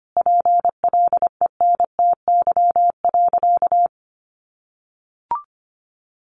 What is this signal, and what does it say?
Morse code, heard as a keyed tone, about 25 wpm, 700 Hz. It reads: PLENTY.
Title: Sentences from Top 1300 Words 25wpm